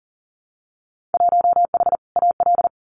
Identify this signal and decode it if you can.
Morse code, heard as a keyed tone, about 40 wpm, 700 Hz. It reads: 1H AL
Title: Winter Field Day Exchanges 40wpm